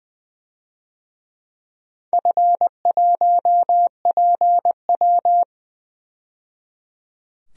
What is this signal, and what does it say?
Morse code, heard as a keyed tone, about 20 wpm, 700 Hz. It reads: F1PW
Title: DXCC-100 Call Signs - 4 Characters - 20wpm